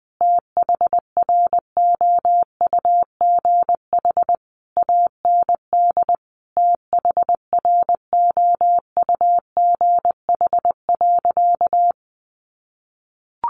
Morse code, heard as a keyed tone, about 20 wpm, 700 Hz. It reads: THROUGH AND THROUGH.
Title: Sentences from Top 1400 Words 20wpm